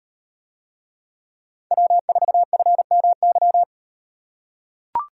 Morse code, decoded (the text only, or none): W4FMY